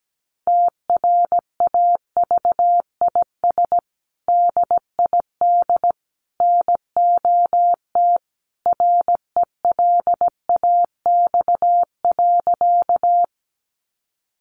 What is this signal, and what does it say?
Morse code, heard as a keyed tone, about 17 wpm, 700 Hz. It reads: TRAVIS DID NOT RELAX.
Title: A Sound Of Thunder 17wpm